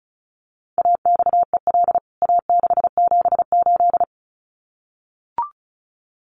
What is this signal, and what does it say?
Morse code, heard as a keyed tone, about 34 wpm, 700 Hz. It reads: AXEL A678
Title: CWT Exchange 35wpm